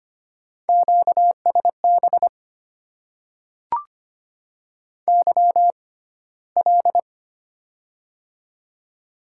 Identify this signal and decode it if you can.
Morse code, heard as a keyed tone, about 25 wpm, spaced as for 15 wpm, 700 Hz. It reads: QSB YL
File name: Top 90 QSO Elements - 6x Character Spacing - 25wpm